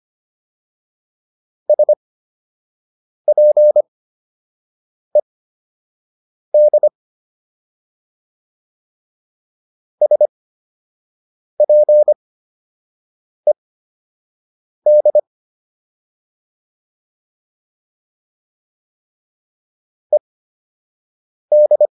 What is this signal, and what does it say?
Morse code, heard as a keyed tone, about 25 wpm, spaced as for 6 wpm, 600 Hz. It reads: SPED SPED ED